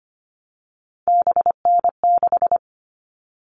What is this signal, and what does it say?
Morse code, heard as a keyed tone, about 25 wpm, 700 Hz. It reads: BN6
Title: All Cut Numbers - 3-Digit 25wpm